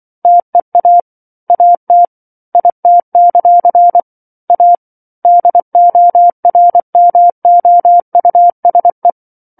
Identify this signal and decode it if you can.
Morse code, heard as a keyed tone, about 24 wpm, 700 Hz. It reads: TEA AT IT; A DORMOUSE
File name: 07-Chapter-07 24 WPM